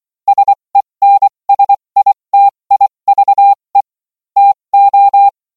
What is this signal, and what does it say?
Morse code, heard as a keyed tone, about 24 wpm, spaced as for 20 wpm, 800 Hz. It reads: SENSITIVE TO